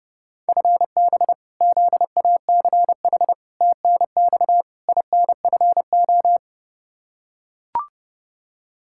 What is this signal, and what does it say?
Morse code, heard as a keyed tone, about 32 wpm, 700 Hz. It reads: FB ZACH TNX INFO
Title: K1USN SST QSO Practice 30wpm